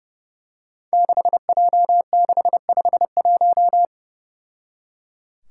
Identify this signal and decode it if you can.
Morse code, heard as a keyed tone, about 30 wpm, 700 Hz. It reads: 6J651
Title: Five Challenging Letter-Number Combinations 30wpm